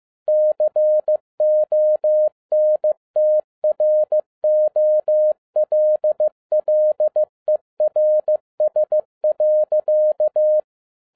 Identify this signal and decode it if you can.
Morse code, heard as a keyed tone, about 15 wpm, 610 Hz. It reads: CONTROLLERS.
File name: 2012-01-mp3 / n17-SUMMARY OF FEA NET IN 2011-15wpm